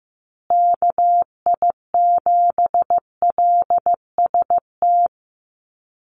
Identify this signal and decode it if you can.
Morse code, heard as a keyed tone, about 15 wpm, 700 Hz. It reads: KI7LST
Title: USA Call Signs Repeated 2x 15wpm